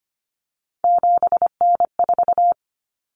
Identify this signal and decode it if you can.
Morse code, heard as a keyed tone, about 25 wpm, 700 Hz. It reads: 7N4
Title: Common Cut Numbers - 3-Digit 25wpm